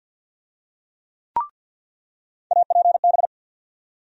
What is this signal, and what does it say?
Morse code, heard as a keyed tone, about 53 wpm, 700 Hz. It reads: APB
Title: USA Call Sign Postfixes - 3 Letters 50wpm